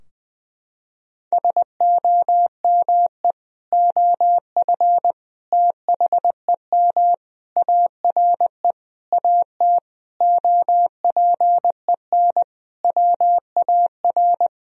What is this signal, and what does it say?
Morse code, heard as a keyed tone, about 21 wpm, 700 Hz. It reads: SOME OF THEM ARE AT OPEN WAR
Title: Sentences from Top 1000 Words - Review Part 2 - 20wpm